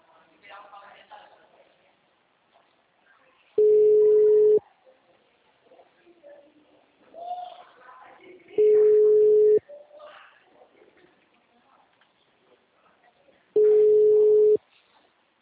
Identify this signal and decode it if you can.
Ringback tone (Europe, 425 Hz), with speech.